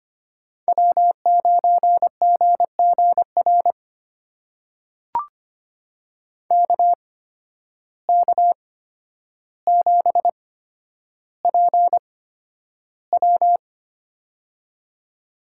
Morse code, heard as a keyed tone, about 25 wpm, spaced as for 11 wpm, 700 Hz. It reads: W9GGR KK7PW
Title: USA Call Signs - 5 Characters - 8x Character Spacing - 25wpm